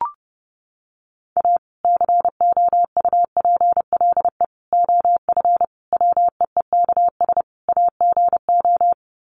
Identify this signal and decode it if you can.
Morse code, heard as a keyed tone, about 30 wpm, 700 Hz. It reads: A COUPLE OF WEEKS AGO
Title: Sets of 5 Words 30wpm